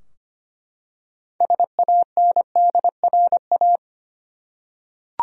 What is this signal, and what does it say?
Morse code, heard as a keyed tone, about 25 wpm, 700 Hz. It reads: SANDRA
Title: USA Common Names - 25wpm